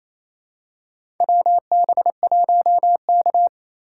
Morse code, heard as a keyed tone, about 27 wpm, 700 Hz. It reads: WB1K
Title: USA Call Signs - 4 Characters - 4x Character Spacing - 28wpm